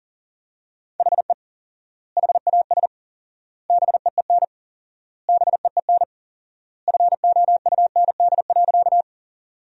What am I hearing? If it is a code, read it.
Morse code, heard as a keyed tone, about 40 wpm, 700 Hz. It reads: HE HAS BEEN BEEN FOUND.
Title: 4x Word-Spacing - Sentences from Top 300 Words - 40wpm